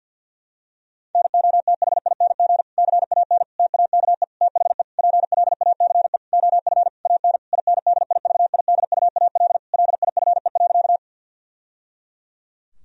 Morse code, heard as a keyed tone, about 50 wpm, 700 Hz. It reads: NOTHING CAN TAKE THE PLACE OF AN INDIVIDUAL LIFE.
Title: Sentences from Top 1000 Words - Review Part 4 - 50wpm